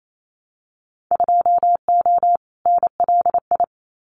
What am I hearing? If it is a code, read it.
Morse code, heard as a keyed tone, about 28 wpm, 700 Hz. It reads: 2O NLI
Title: Winter Field Day Exchanges 28wpm